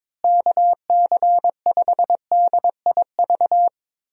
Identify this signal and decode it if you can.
Morse code, heard as a keyed tone, about 22 wpm, 700 Hz. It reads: KC5DIV
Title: USA Call Signs - 6 Characters - Repeated 2x - 22wpm